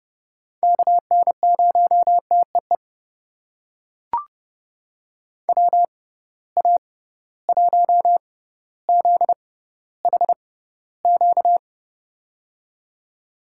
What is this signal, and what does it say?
Morse code, heard as a keyed tone, about 30 wpm, spaced as for 15 wpm, 700 Hz. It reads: KN0TEE WA1ZHQ
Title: USA Call Signs - 6 Characters - 6x Character Spacing - 30wpm